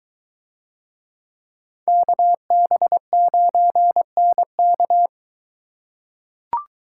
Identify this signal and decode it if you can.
Morse code, heard as a keyed tone, about 22 wpm, 700 Hz. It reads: KB9NK